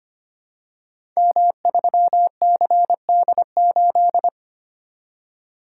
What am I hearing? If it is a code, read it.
Morse code, heard as a keyed tone, about 25 wpm, 700 Hz. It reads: M3CD8
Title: Code Groups - 5 Letter:Numbers - 25wpm